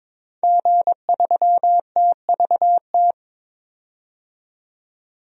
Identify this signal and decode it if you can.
Morse code, heard as a keyed tone, about 22 wpm, 700 Hz. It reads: G3TVT
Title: DXCC-100 Call Signs - 5 Characters - 22wpm